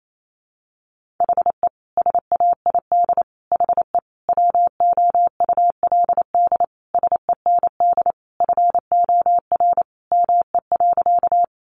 Morse code, heard as a keyed tone, about 28 wpm, 700 Hz. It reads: HE SAID HE WOULD SEND FOR ME.
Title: Sentences from Top 1500 Words 28wpm